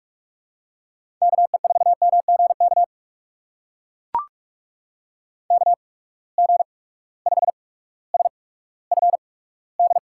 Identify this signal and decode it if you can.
Morse code, heard as a keyed tone, about 44 wpm, spaced as for 19 wpm, 700 Hz. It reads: KE4MGK KG5SFD
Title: USA Call Signs - 6 Characters - 8x Character Spacing - 45wpm